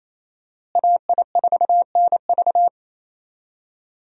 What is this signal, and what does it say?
Morse code, heard as a keyed tone, about 28 wpm, 700 Hz. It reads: AI4NV
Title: USA Call Signs - 5 Characters - Repeated 2x - 28wpm